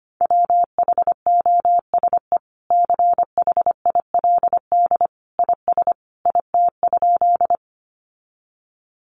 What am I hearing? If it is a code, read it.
Morse code, heard as a keyed tone, about 25 wpm, 700 Hz. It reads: WHOSE CHILD IS IT?
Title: Sentences from Top 1200 Words 25wpm